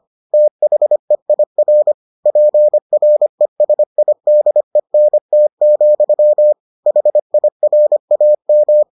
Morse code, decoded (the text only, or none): THEIR PRESIDENT, HIRAM